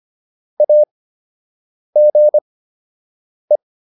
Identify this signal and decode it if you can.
Morse code, heard as a keyed tone, about 25 wpm, spaced as for 7 wpm, 600 Hz. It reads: AGE